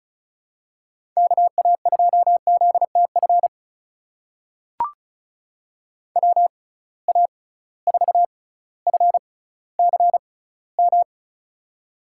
Morse code, heard as a keyed tone, about 36 wpm, spaced as for 18 wpm, 700 Hz. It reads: KA2ZTF WA4FCM
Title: USA Call Signs - 6x Character Spacing - 35wpm